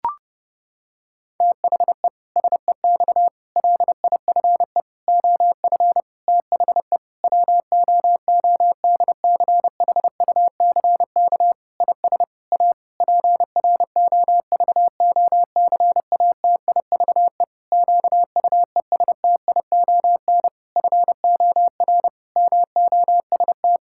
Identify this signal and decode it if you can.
Morse code, heard as a keyed tone, about 28 wpm, 700 Hz. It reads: THE SEX LIFE OF THE WOODCHUCK IS A PROVOCATIVE QUESTION FOR MOST
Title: Pangram Challenge v2 - 30wpm